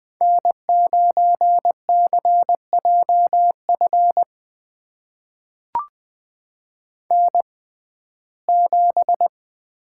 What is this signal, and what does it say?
Morse code, heard as a keyed tone, about 20 wpm, spaced as for 14 wpm, 700 Hz. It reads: N9CJF N7
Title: USA Call Signs - 5 Characters - 6x Character Spacing - 20wpm